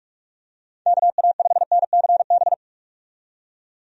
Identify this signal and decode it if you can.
Morse code, heard as a keyed tone, about 41 wpm, 700 Hz. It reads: KA5NCB
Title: USA Call Signs - 6 Characters - 45wpm